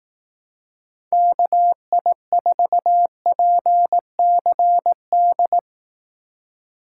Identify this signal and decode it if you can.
Morse code, heard as a keyed tone, about 18 wpm, 700 Hz. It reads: KI4PCD
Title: Speed Racing - USA 2x3 Call Signs 15wpm